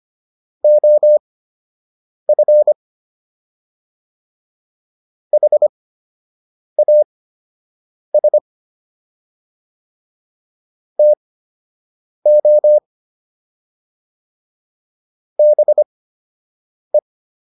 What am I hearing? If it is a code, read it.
Morse code, heard as a keyed tone, about 25 wpm, spaced as for 7 wpm, 600 Hz. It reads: OF HAS TO BE